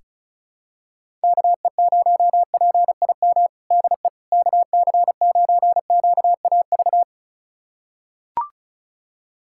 Morse code, heard as a keyed tone, about 36 wpm, 700 Hz. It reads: KE0PIM DE KC9QAV
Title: USA Callsign DE Callsign - 6 Characters - 35wpm